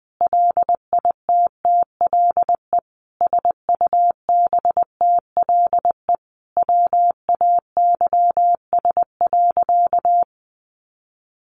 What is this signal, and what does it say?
Morse code, heard as a keyed tone, about 20 wpm, 700 Hz. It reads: LITTLE SUBTLE WAYS.
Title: A Sound Of Thunder 20wpm